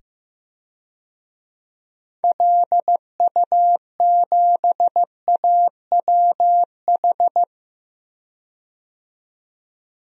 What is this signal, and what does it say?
Morse code, heard as a keyed tone, about 15 wpm, 700 Hz. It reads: LU7AWH
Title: DXCC-100 Call Signs - 6 Characters - 15wpm